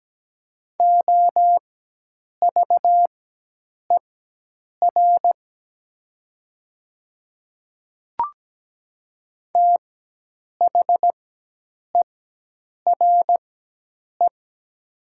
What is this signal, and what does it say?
Morse code, heard as a keyed tone, about 17 wpm, spaced as for 8 wpm, 700 Hz. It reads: OVER THERE
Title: 4x Character Spacing - Top 100 Words - 17wpm